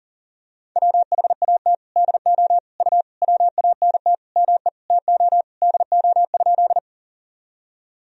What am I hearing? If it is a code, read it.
Morse code, heard as a keyed tone, about 40 wpm, 700 Hz. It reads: WHAT DO U WANT ME TO DO?
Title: Sentences from Top 200 Words and Abbreviations 40wpm